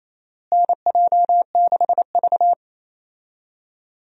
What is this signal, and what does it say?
Morse code, heard as a keyed tone, about 28 wpm, 700 Hz. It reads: NJ6V
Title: USA Call Signs - 4 Characters - Repeated 2x - 28wpm